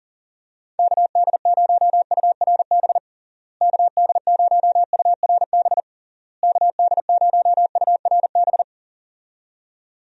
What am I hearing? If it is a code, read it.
Morse code, heard as a keyed tone, about 40 wpm, 700 Hz. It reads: KD0URB KD0URB KD0URB
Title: USA Call Signs - 6 Characters - Repeated 3x - 40wpm